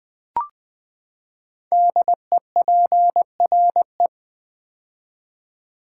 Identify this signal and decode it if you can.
Morse code, heard as a keyed tone, about 20 wpm, 700 Hz. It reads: DEPRE
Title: Build A Word - 5 Letters At A Time - 10-Letter Words - 20wpm